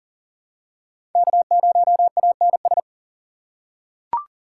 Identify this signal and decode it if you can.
Morse code, heard as a keyed tone, about 40 wpm, 700 Hz. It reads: K0ANS
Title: USA Call Signs - 5 Characters - 3x Character Spacing - 40wpm